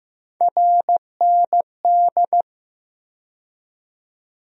Morse code, heard as a keyed tone, about 15 wpm, 700 Hz. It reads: RND
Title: USA Call Sign Postfixes - 3 Letters 15wpm